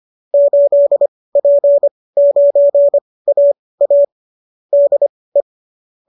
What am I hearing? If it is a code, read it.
Morse code, heard as a keyed tone, about 25 wpm, spaced as for 18 wpm, 570 Hz. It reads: 8P9AA DE